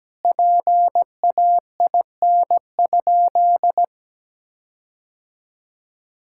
Morse code, heard as a keyed tone, about 17 wpm, 700 Hz. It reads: PAIN?
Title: Sentences from Top 1600 Words 17wpm